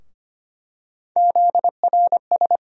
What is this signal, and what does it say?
Morse code, heard as a keyed tone, about 25 wpm, 700 Hz. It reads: ZRS
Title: USA Call Sign Postfixes - 3 Letters 25wpm